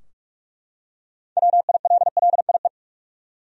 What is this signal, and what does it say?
Morse code, heard as a keyed tone, about 43 wpm, 700 Hz. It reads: WILLIE
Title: USA Common Names - 45wpm